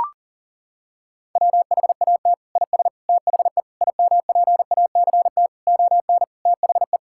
Morse code, heard as a keyed tone, about 37 wpm, 700 Hz. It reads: WHAT IS THE IMPACT ON THE
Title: Sentences from Top 1000 Words - Review Part 6 - 40wpm